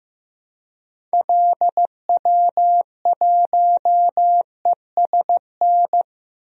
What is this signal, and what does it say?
Morse code, heard as a keyed tone, about 15 wpm, 700 Hz. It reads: LW1ESN
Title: DXCC-100 Call Signs - 6 Characters - 15wpm